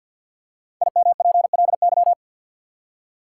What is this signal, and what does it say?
Morse code, heard as a keyed tone, about 53 wpm, 700 Hz. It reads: IMPLY
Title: Five Letter Words - 50wpm